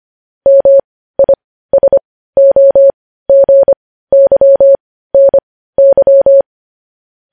Morse code, heard as a keyed tone, about 25 wpm, spaced as for 15 wpm, 550 Hz. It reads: MISOGYNY
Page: news headlines in morse code at 15 wpm